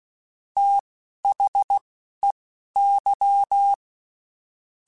Morse code, heard as a keyed tone, about 16 wpm, spaced as for 11 wpm, 800 Hz. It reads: THEY
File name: Quotes Spaced / Quotes Spaced-mp3s / Advice sp @15